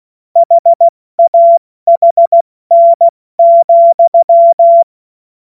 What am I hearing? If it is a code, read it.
Morse code, heard as a keyed tone, about 16 wpm, 680 Hz. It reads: HAHN,
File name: Witz-15-Juli-2025-70-F80